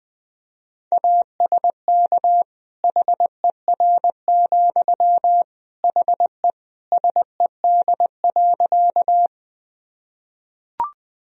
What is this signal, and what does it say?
Morse code, heard as a keyed tone, about 20 wpm, 700 Hz. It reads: ASK HER, HE SED.
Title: Sentences from Top 200 Words and Abbreviations 20wpm